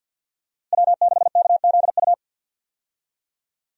Morse code, heard as a keyed tone, about 49 wpm, 700 Hz. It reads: W6KZU
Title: USA Call Signs - 5 Characters - 50wpm